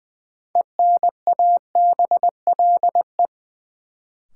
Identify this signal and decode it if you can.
Morse code, heard as a keyed tone, about 20 wpm, 700 Hz. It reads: ENABLE
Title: Sentences from Top 1500 Words - New Word Drill Alt - 20wpm